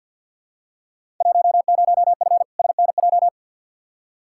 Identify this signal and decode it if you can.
Morse code, heard as a keyed tone, about 46 wpm, 700 Hz. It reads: 10F SNJ
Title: Field Day Exchanges Encore - 50wpm